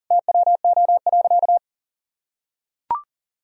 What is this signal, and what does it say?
Morse code, heard as a keyed tone, about 40 wpm, 700 Hz. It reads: TWO.
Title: Sentences from Top 1000 Words - Review Part 2 - 40wpm